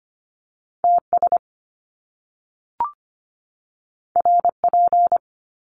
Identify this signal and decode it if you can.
Morse code, heard as a keyed tone, about 25 wpm, 700 Hz. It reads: TS RP
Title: Top Two Letter Combinations in Words 25wpm